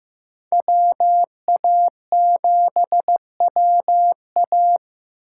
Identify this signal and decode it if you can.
Morse code, heard as a keyed tone, about 15 wpm, 700 Hz. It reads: WA7WA
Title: USA Call Signs - 5 Characters - 4x Character Spacing - 15wpm